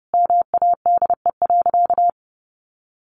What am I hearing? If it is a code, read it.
Morse code, heard as a keyed tone, about 29 wpm, 700 Hz. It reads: MADE.
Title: Sentences from Top 1400 Words 30wpm